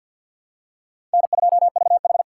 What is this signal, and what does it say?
Morse code, heard as a keyed tone, about 52 wpm, 700 Hz. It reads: N2VH
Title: USA Call Signs - 4 Characters - 2x Character Spacing - 50wpm